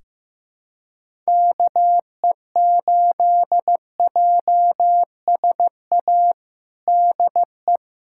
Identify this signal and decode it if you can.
Morse code, heard as a keyed tone, about 15 wpm, 700 Hz. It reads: KE8JSA DE
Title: USA Callsign DE Callsign - 6 Characters - 15wpm